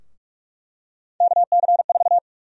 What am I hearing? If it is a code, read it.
Morse code, heard as a keyed tone, about 46 wpm, 700 Hz. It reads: KC4
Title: USA Call Sign Prefixes - 45wpm